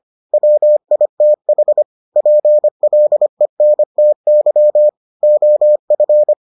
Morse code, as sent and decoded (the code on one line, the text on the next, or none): .-- .. - .... / .--. .-.. -. - -.-- / --- ..-.
WITH PLENTY OF